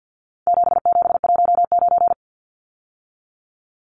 Morse code, heard as a keyed tone, about 50 wpm, 700 Hz. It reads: 7719
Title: Numbers - 4-Digit 50wpm